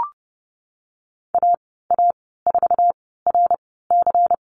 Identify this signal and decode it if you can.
Morse code, heard as a keyed tone, about 30 wpm, spaced as for 17 wpm, 700 Hz. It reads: AA4RC
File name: Lesson 19g - Letter w - Call Signs 30wpm